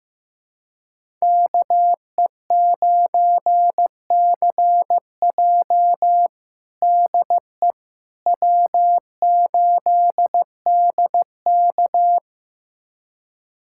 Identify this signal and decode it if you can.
Morse code, heard as a keyed tone, about 16 wpm, 700 Hz. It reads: KE9CJ DE W8DK
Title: USA Callsign DE Callsign 15wpm